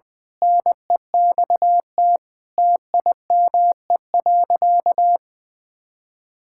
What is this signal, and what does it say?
Morse code, heard as a keyed tone, about 21 wpm, 700 Hz. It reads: NEXT TIME.
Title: Sentences from Top 1000 Words - Review Part 6 - 20wpm